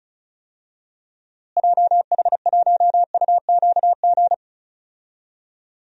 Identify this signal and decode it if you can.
Morse code, heard as a keyed tone, about 36 wpm, 700 Hz. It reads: JH1UQG
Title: DXCC-100 Call Signs - 6 Characters - Repeated 3x - 35wpm